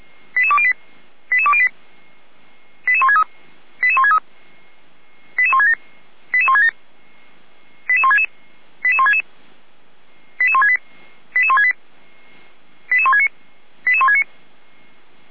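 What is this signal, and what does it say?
Five-tone selective call (ZVEI), heard as quick runs of five tones, tones of about 70 ms. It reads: 88208 88208 88163 88163 88167 88167 88166 88166 88168 88168 88169 88169
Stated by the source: Five-tone ZVEI selective call received on SDR.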